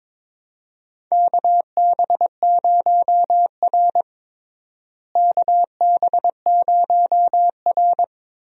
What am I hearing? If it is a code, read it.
Morse code, heard as a keyed tone, about 21 wpm, 700 Hz. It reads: KB0R KB0R
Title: USA Call Signs - 4 Characters - Repeated 3x - 22wpm